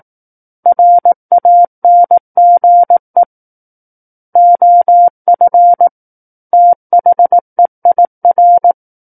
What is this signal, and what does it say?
Morse code, heard as a keyed tone, about 18 wpm, 700 Hz. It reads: RANGE OF THEIR
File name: Book One / 09-Chapter 9 - 18 WPM